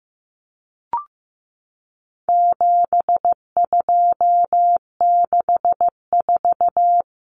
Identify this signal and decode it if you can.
Morse code, heard as a keyed tone, about 15 wpm, 700 Hz. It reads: 7264
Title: Numbers - 4-Digit 15wpm